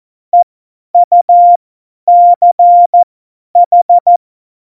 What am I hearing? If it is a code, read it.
Morse code, heard as a keyed tone, about 14 wpm, spaced as for 10 wpm, 700 Hz. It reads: EUCH
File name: Morsekurs / Witze / Witz 01. Dezember 2020, 50 F70